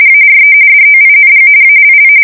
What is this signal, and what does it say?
RTTY, about 45.45 baud. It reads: ARE DOING ALL
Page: 170 Shift 2125/2295